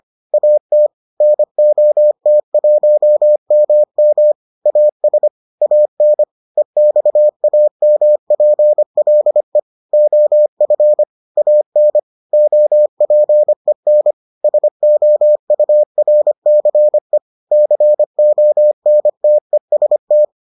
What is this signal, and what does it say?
Morse code, heard as a keyed tone, about 25 wpm, 600 Hz. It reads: AT NOT1MM AS AN EXAMPLE OF AN OPEN SOURCE CONTEST